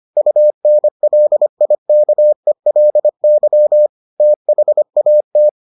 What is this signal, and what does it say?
Morse code, heard as a keyed tone, about 25 wpm, 600 Hz. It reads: UNLIKELY THAT